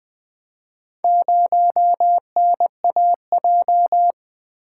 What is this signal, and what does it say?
Morse code, heard as a keyed tone, about 20 wpm, 700 Hz. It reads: 0NAJ
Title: Four Letter-Number Combinations - 20wpm